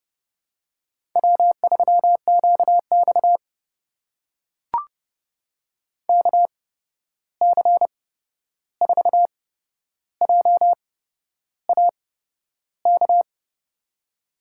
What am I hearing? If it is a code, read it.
Morse code, heard as a keyed tone, about 30 wpm, spaced as for 11 wpm, 700 Hz. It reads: W3QX KC4JAK
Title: USA Call Signs - 8x Character Spacing - 30wpm